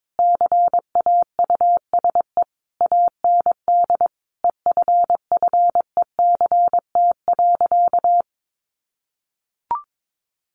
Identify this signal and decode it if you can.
Morse code, heard as a keyed tone, about 22 wpm, 700 Hz. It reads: CAUSE AND EFFECT.
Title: Sentences from Top 1000 Words 22wpm